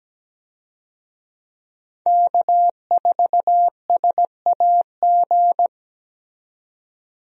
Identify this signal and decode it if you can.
Morse code, heard as a keyed tone, about 17 wpm, 700 Hz. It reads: K4SAG
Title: USA Call Signs - 5 Characters - 17wpm